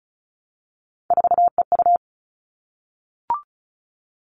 Morse code, heard as a keyed tone, about 36 wpm, 700 Hz. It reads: 4EU
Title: Three Challenging Letter-Number Combinations 35wpm